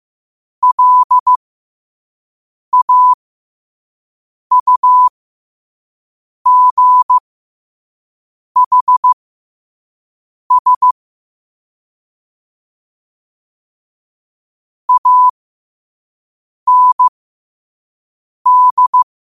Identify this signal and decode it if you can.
Morse code, heard as a keyed tone, about 15 wpm, spaced as for 5 wpm, 1000 Hz. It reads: LAUGHS AND
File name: 2024-10-07-5wpm